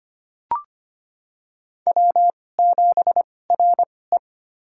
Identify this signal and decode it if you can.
Morse code, heard as a keyed tone, about 25 wpm, spaced as for 18 wpm, 700 Hz. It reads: W7RE
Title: USA Call Signs - 2x Character Spacing - 25wpm